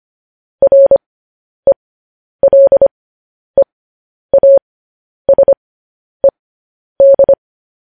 Morse code, heard as a keyed tone, about 25 wpm, spaced as for 10 wpm, 550 Hz. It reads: RELEASED
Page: news-headlines-in-morse-code-at-10-wpm